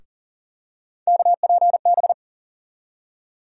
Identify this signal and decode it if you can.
Morse code, heard as a keyed tone, about 40 wpm, 700 Hz. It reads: KPB
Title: USA Call Sign Postfixes - 3 Letters 40wpm